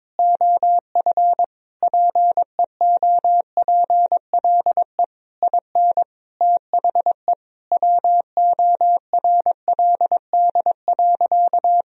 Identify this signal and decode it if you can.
Morse code, heard as a keyed tone, about 22 wpm, 700 Hz. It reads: OF PEOPLE IN THE WORLD.
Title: Sentences from Top 1000 Words - Review Part 4 - 22wpm